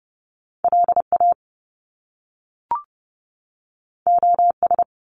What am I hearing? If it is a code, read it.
Morse code, heard as a keyed tone, about 30 wpm, 700 Hz. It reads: LA OS
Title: Top Two Letter Combinations in Words 30wpm